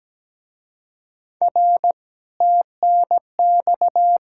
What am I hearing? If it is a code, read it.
Morse code, heard as a keyed tone, about 20 wpm, 700 Hz. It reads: R TNX
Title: Speed-Racing Common QSO Phrases - 17wpm